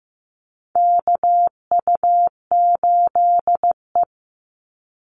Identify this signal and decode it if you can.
Morse code, heard as a keyed tone, about 15 wpm, 700 Hz. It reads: KU8E
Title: Call Signs Repeated 3x - 15wpm